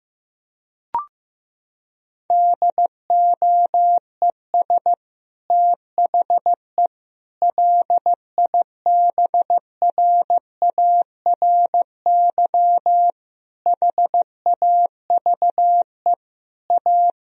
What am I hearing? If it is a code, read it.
Morse code, heard as a keyed tone, about 15 wpm, 700 Hz. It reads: DOES THE LIBRARY HAVE A
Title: Sentences from Top 1000 Words - Review Part 4 - 15wpm